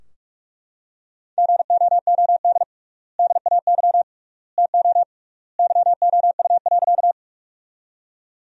Morse code, heard as a keyed tone, about 43 wpm, 700 Hz. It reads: GOOD DAY TO YOU.
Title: 3x Word-Spacing - Sentences from Top 200 Words - 45wpm